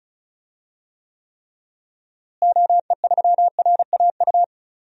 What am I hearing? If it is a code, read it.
Morse code, heard as a keyed tone, about 34 wpm, 700 Hz. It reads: OE3RAU
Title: DXCC-100 Call Signs - 6 Characters - Repeated 3x - 35wpm